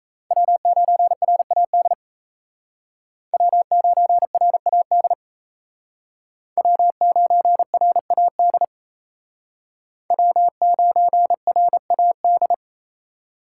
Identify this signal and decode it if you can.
Morse code, heard as a keyed tone, about 36 wpm, 700 Hz. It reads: W9RAD W9RAD W9RAD W9RAD